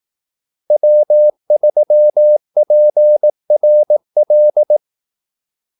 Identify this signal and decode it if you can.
Morse code, heard as a keyed tone, about 18 wpm, 600 Hz. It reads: W3PRL
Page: Kaldesignaler 18 wpm